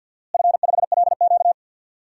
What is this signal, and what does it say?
Morse code, heard as a keyed tone, about 51 wpm, 700 Hz. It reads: F5LQ
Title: DXCC-100 Call Signs - 4 Characters - Repeated 3x - 50wpm